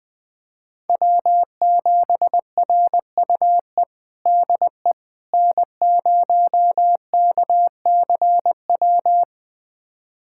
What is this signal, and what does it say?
Morse code, heard as a keyed tone, about 20 wpm, 700 Hz. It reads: W7RUE DE N0KCW
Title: USA Callsign DE Callsign - 5 Characters - 20wpm